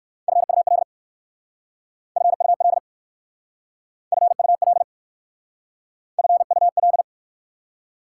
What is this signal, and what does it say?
Morse code, heard as a keyed tone, about 59 wpm, 700 Hz. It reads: FUL FUL FUL FUL